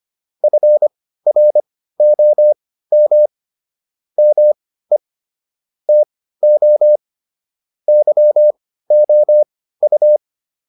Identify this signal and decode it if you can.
Morse code, heard as a keyed tone, about 25 wpm, spaced as for 15 wpm, 600 Hz. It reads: FROM ME TO YOU
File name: PR302 15